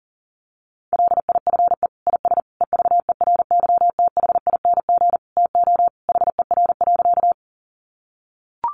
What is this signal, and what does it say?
Morse code, heard as a keyed tone, about 40 wpm, 700 Hz. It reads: LIFE IS EVERYTHING TO HER.
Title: Sentences from Top 700 Words 40wpm